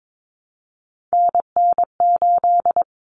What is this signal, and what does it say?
Morse code, heard as a keyed tone, about 22 wpm, 700 Hz. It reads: NN8
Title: Common Cut Numbers - 3-Digit 22wpm